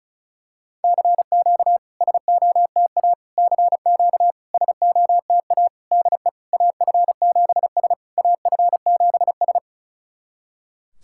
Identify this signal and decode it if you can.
Morse code, heard as a keyed tone, about 36 wpm, 690 Hz. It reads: CQ SOTA CQ SOTA DE AF7S AF7S
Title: SOTA QSO Practice 35wpm